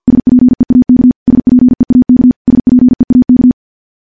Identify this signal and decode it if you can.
Morse code, heard as a keyed tone, about 50 wpm, 250 Hz. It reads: SPEAK SPEAK SPEAK
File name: morse-training-50WPM